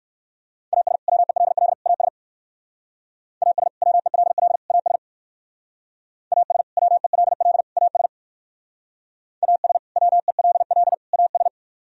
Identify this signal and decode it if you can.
Morse code, heard as a keyed tone, about 63 wpm, 700 Hz. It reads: AS WELL AS AS WELL AS AS WELL AS AS WELL AS